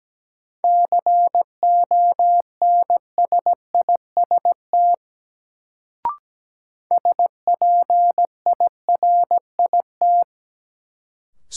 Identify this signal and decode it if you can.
Morse code, heard as a keyed tone, about 17 wpm, 700 Hz. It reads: CONSIST SPIRIT
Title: Sentences from Top 1400 Words - New Word Drill Alt - 17wpm